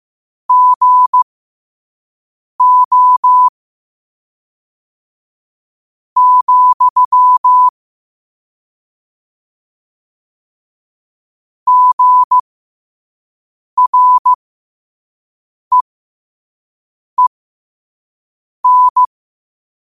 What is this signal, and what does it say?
Morse code, heard as a keyed tone, about 15 wpm, spaced as for 5 wpm, 1000 Hz. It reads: GO, GREEN
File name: qotdfiles / 5wpm